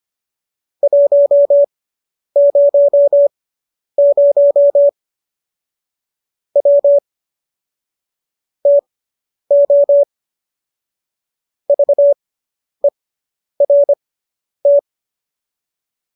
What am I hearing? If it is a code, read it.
Morse code, heard as a keyed tone, about 25 wpm, spaced as for 10 wpm, 570 Hz. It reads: 100 W TO VERT